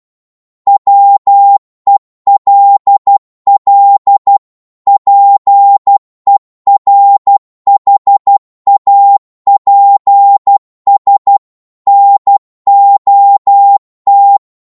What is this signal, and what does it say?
Morse code, heard as a keyed tone, about 12 wpm, 800 Hz. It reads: WELL PERHAPS NOT
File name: Morse / psg-12wpm